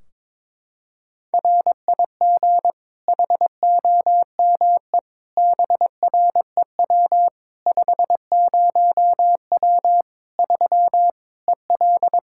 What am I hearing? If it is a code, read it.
Morse code, heard as a keyed tone, about 22 wpm, 700 Hz. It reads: RIG HOME BREW 50W 3 EL
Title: Practice QSO - Exchange 2 - Info - 22wpm